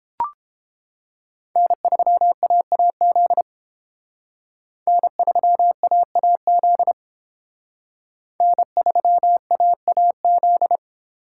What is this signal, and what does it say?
Morse code, heard as a keyed tone, about 30 wpm, 700 Hz. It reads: N3AAZ N3AAZ N3AAZ